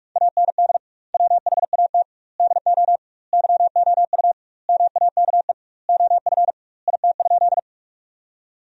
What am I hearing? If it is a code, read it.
Morse code, heard as a keyed tone, about 46 wpm, 700 Hz. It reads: AND WHAT DO YOU MAKE OF IT?
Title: 2x Word-Spacing - Sentences from Top 100 Words - 45wpm